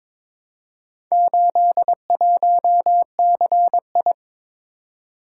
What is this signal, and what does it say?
Morse code, heard as a keyed tone, about 22 wpm, 700 Hz. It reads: 81CI
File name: practice / 22wpm / Four Letter-Number Combinations - 22wpm